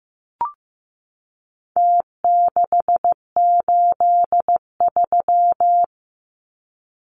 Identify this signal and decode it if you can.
Morse code, heard as a keyed tone, about 15 wpm, 700 Hz. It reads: T683
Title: Numbers - Contest Style - 4-Digit 15wpm